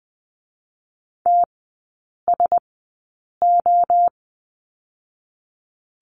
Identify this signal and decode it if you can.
Morse code, heard as a keyed tone, about 20 wpm, 700 Hz. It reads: T S O
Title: 047 Lesson 9e - Number 4 - Three Characters 20wpm